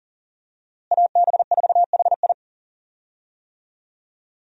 Morse code, heard as a keyed tone, about 40 wpm, 700 Hz. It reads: AB4HI
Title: USA Call Signs - 5 Characters - Repeated 3x - 40wpm